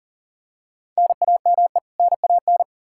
Morse code, heard as a keyed tone, about 40 wpm, 700 Hz. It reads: NAME NAN
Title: Practice QSO - Exchange 1 - Beginning - 40wpm